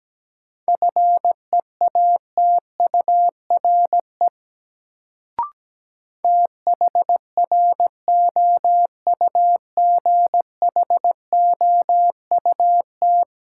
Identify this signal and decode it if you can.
Morse code, heard as a keyed tone, about 17 wpm, 700 Hz. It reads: FEATURE THROUGHOUT
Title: Sentences from Top 800 Words - New Word Drill - 17wpm